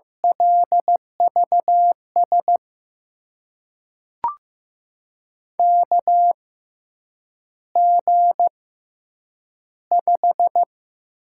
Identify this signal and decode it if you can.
Morse code, heard as a keyed tone, about 16 wpm, spaced as for 8 wpm, 700 Hz. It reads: LVS KG5
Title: USA Call Signs - 6 Characters - 6x Character Spacing - 15wpm